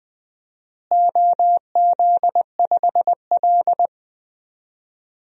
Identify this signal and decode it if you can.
Morse code, heard as a keyed tone, about 20 wpm, 700 Hz. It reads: OZ5L
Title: DXCC-100 Call Signs - 4 Characters - Repeated 3x - 20wpm